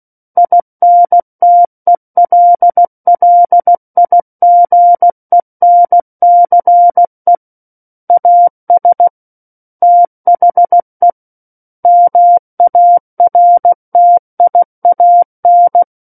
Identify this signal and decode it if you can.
Morse code, heard as a keyed tone, about 16 wpm, 700 Hz. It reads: INTELLIGENCE AS THE MARTIAN
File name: Book One / 07-Chapter 7 - 16 WPM